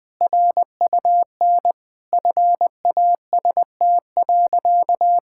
Morse code, heard as a keyed tone, about 21 wpm, 700 Hz. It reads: RUN FAST.
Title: Sentences with 2-Word Phrases - 20wpm